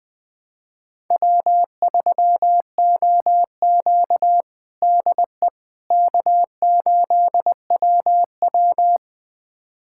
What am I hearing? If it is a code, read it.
Morse code, heard as a keyed tone, about 20 wpm, 700 Hz. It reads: W3OQ DE K8WW
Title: USA Callsign DE Callsign - 4 Characters - 20wpm